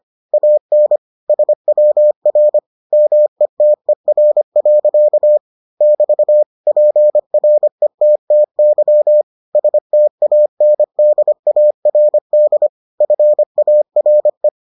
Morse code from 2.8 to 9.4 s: METER. <BT> PRETTY

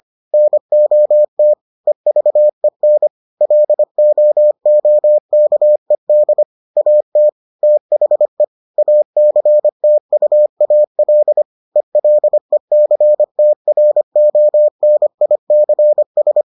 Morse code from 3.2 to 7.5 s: LOOKED AT

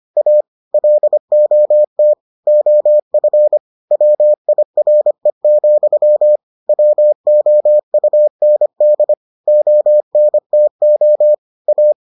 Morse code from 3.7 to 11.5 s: WIRE, WOUND ONTO

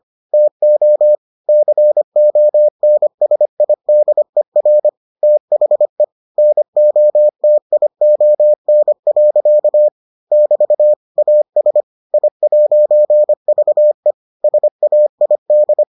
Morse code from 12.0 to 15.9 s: I'VE SAID

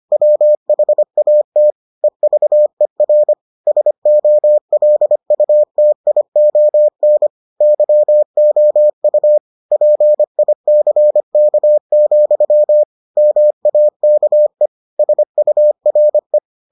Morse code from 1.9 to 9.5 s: EVER SOLUTION YOU

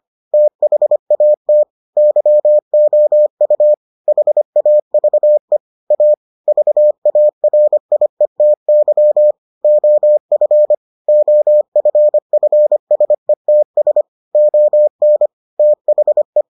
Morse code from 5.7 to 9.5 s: A VARIETY

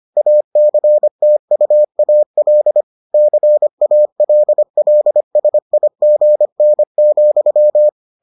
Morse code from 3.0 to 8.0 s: CALLSIGN,

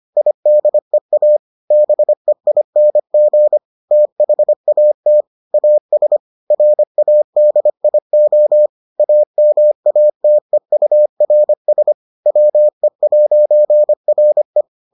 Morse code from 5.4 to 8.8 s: AS RADIO

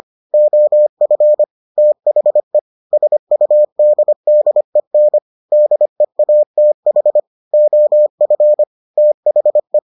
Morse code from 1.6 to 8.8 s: THE SUDDEN DEATH OF

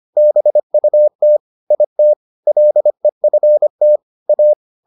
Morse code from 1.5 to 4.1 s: IT LEFT

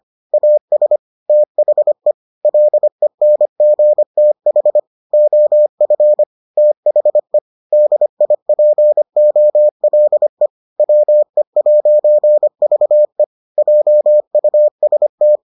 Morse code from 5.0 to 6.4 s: OF

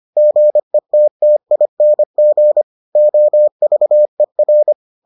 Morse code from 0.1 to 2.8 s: GETTING

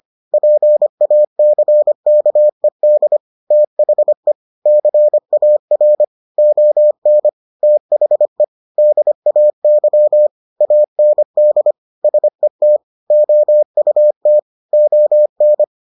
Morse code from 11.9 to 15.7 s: SET OUT ON

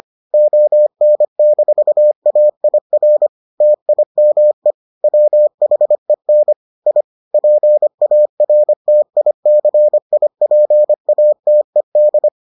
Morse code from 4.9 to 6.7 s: WHEN

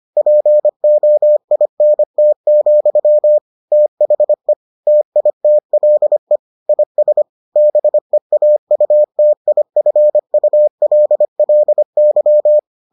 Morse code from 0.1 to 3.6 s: POINT,

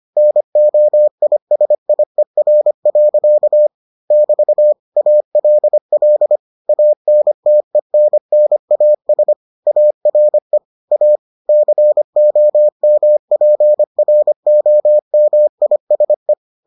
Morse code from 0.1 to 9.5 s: NOISIER. <BT> ALL ANTENNAS

